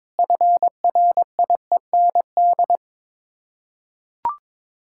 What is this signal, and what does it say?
Morse code, heard as a keyed tone, about 21 wpm, 700 Hz. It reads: FRIEND
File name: practice / 22wpm / Sets of 8 Words - 22wpm